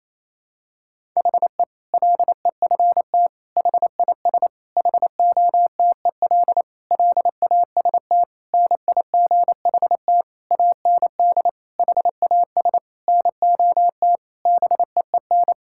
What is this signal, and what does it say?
Morse code, heard as a keyed tone, about 28 wpm, 700 Hz. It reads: HE LEFT HIS HOTEL LAST NIGHT AND HAS NOT BEEN
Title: Sentences from Top 1600 Words 28wpm